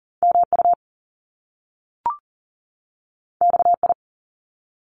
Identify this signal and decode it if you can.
Morse code, heard as a keyed tone, about 40 wpm, 700 Hz. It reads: MU XI
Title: Top Two Letter Combinations in Words 40wpm